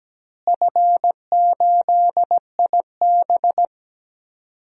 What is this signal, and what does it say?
Morse code, heard as a keyed tone, about 17 wpm, 700 Hz. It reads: F8IB
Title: DXCC-100 Call Signs - 4 Characters - Repeated 3x - 17wpm